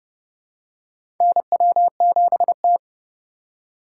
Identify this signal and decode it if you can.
Morse code, heard as a keyed tone, about 29 wpm, 700 Hz. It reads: NW7T
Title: USA Call Signs - 4 Characters - 4x Character Spacing - 30wpm